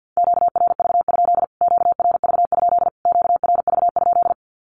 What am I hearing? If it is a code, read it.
Morse code, heard as a keyed tone, about 50 wpm, 700 Hz. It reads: QRV? QRV? QRV?
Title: Q-Codes Repeated 3x 50wpm